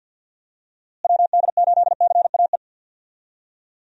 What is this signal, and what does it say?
Morse code, heard as a keyed tone, about 47 wpm, 700 Hz. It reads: WD8CAE
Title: USA Call Signs - 6 Characters - Repeated 3x - 50wpm